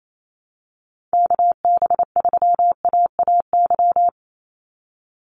Morse code, heard as a keyed tone, about 28 wpm, 700 Hz. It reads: KB3AAY
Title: Call Signs - 28wpm